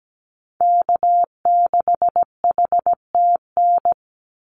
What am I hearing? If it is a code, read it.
Morse code, heard as a keyed tone, about 17 wpm, 700 Hz. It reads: K6HTN
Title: Call Signs Repeated 3x - 17wpm